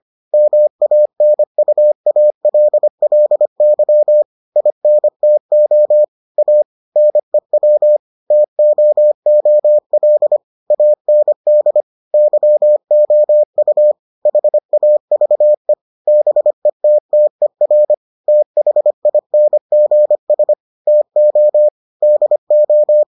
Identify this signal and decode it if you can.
Morse code, heard as a keyed tone, about 25 wpm, 600 Hz. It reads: MANUALLY INTO A NEW TOOL AND YOU HAVE BETTER THINGS TO DO